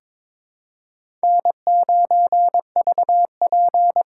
Morse code, heard as a keyed tone, about 22 wpm, 700 Hz. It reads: N9VP
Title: USA Call Signs - 4 Characters - 2x Character Spacing - 22wpm